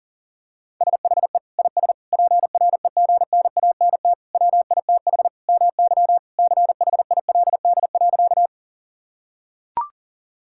Morse code, heard as a keyed tone, about 41 wpm, 700 Hz. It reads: SHE IS PREGNANT WITH MY CHILD.
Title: Sentences from Top 1000 Words - Review Part 6 - 40wpm